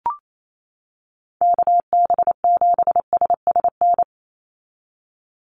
Morse code, heard as a keyed tone, about 28 wpm, 700 Hz. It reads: KB7SSN
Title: USA Call Signs 28wpm